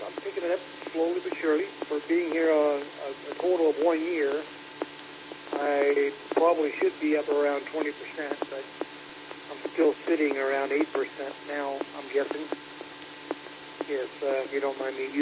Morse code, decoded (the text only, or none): none